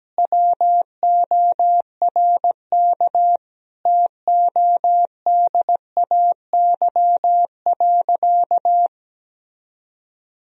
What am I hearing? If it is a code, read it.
Morse code, heard as a keyed tone, about 17 wpm, 700 Hz. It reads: WORK TODAY.
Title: Sentences from Top 1000 Words - Review Part 4 - 17wpm